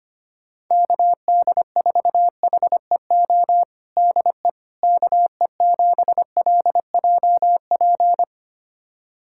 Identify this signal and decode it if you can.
Morse code, heard as a keyed tone, about 25 wpm, 700 Hz. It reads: KD4HEO DE KE7LJP
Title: USA Callsign DE Callsign 25wpm